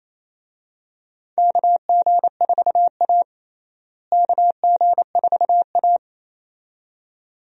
Morse code, heard as a keyed tone, about 28 wpm, 700 Hz. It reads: KG4A KG4A
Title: USA Call Signs - 4 Characters - Repeated 2x - 28wpm